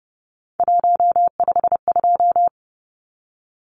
Morse code, heard as a keyed tone, about 30 wpm, 700 Hz. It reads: 152
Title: Numbers - 3-Digit 30wpm